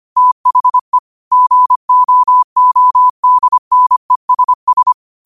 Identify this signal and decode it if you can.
Morse code, heard as a keyed tone, about 25 wpm, 1000 Hz. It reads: THE GOODNESS
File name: qotdfiles / 25wpm